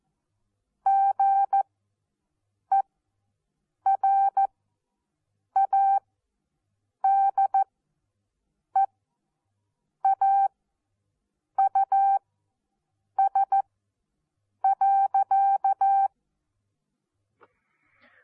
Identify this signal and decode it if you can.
Morse code, heard as a keyed tone, about 14 wpm, spaced as for 6 wpm, 780 Hz. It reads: GERADEAUS.
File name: Onlinelehrgang / morsen / DJ4UF Morsen14